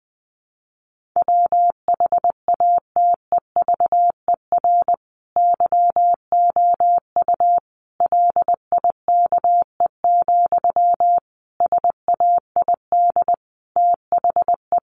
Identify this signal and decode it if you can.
Morse code, heard as a keyed tone, about 20 wpm, 700 Hz. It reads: WHATEVER YOU LIKE, SAID THE
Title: Sentences from Top 1100 Words 20wpm